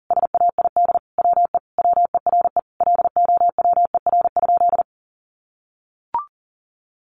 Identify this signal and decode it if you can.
Morse code, heard as a keyed tone, about 40 wpm, 700 Hz. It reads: SAID WE WERE LOWER?
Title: Sentences from Top 700 Words 40wpm